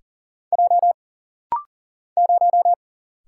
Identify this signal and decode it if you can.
Morse code, heard as a keyed tone, about 40 wpm, 700 Hz. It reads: J 0
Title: Single Letter-Number - Rapid-Fire 40wpm